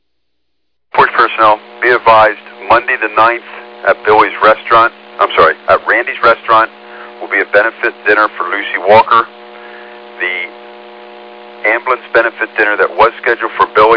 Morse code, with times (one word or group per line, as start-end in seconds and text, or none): none